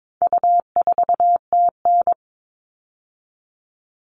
Morse code, heard as a keyed tone, about 22 wpm, 700 Hz. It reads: U4TN
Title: All Cut Numbers - 4-Digit 22wpm